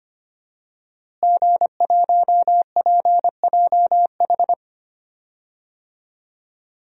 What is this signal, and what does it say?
Morse code, heard as a keyed tone, about 25 wpm, 700 Hz. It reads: G1PJH
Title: DXCC-100 Call Signs - 5 Characters - Repeated 3x - 25wpm